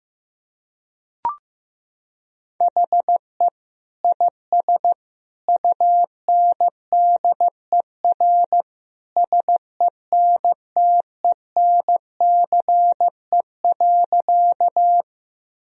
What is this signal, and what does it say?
Morse code, heard as a keyed tone, about 15 wpm, 700 Hz. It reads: HE IS UNDER SENTENCE.
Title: Sentences from Top 1000 Words - Review Part 1 - 15wpm